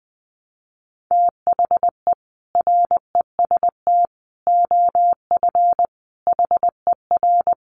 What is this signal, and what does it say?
Morse code, heard as a keyed tone, about 19 wpm, 700 Hz. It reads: THE REST OF HER
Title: Sets of 5 Words Encore 20wpm